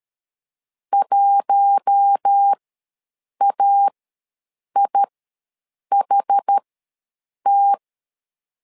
Morse code, heard as a keyed tone, about 13 wpm, spaced as for 7 wpm, 800 Hz. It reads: 1AIHT
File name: LEZIONI / 4 Q1 ws 12wpm 2129